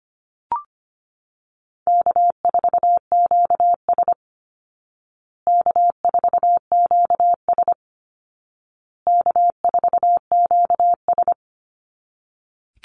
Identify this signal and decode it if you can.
Morse code, heard as a keyed tone, about 25 wpm, 700 Hz. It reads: K4QS K4QS K4QS
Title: Call Signs Repeated 2x - 25wpm